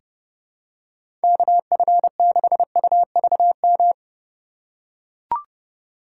Morse code, heard as a keyed tone, about 30 wpm, 700 Hz. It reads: KF6UVM
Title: USA Call Signs - 2x Character Spacing - 30wpm